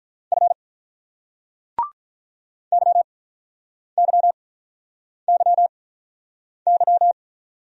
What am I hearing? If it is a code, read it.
Morse code, heard as a keyed tone, about 46 wpm, spaced as for 9 wpm, 700 Hz. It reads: F YYYY